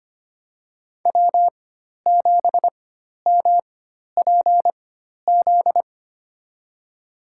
Morse code, heard as a keyed tone, about 25 wpm, spaced as for 12 wpm, 700 Hz. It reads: W7MPZ
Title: USA Call Signs - 5 Characters - 4x Character Spacing - 25wpm